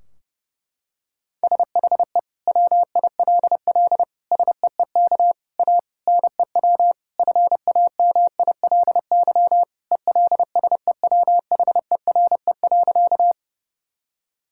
Morse code, heard as a keyed tone, about 32 wpm, 700 Hz. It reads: SHE WILL SEEK A NEW FAMILY ELSEWHERE.
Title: Sentences from Top 1600 Words 30wpm